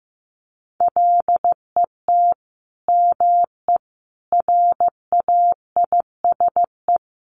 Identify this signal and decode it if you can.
Morse code, heard as a keyed tone, about 15 wpm, 700 Hz. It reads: LET ME RAISE
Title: Sentences from Top 1400 Words 15wpm